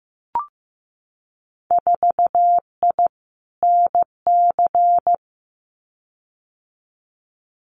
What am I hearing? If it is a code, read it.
Morse code, heard as a keyed tone, about 15 wpm, 700 Hz. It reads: 4I NC
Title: Winter Field Day Exchanges 15wpm